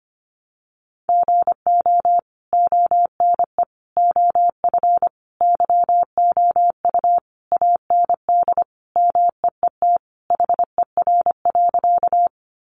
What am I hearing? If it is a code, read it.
Morse code, heard as a keyed tone, about 25 wpm, 700 Hz. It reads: GO ONE OF YOU AND MEET HER.
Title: Sentences from Top 1100 Words 25wpm